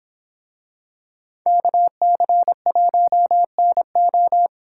Morse code, heard as a keyed tone, about 26 wpm, 700 Hz. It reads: KC1NO
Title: Speed Racing - USA 2x2 Call Signs 22wpm